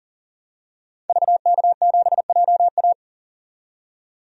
Morse code, heard as a keyed tone, about 37 wpm, 700 Hz. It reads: VK7JA
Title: DXCC-100 Call Signs - 5 Characters - 40wpm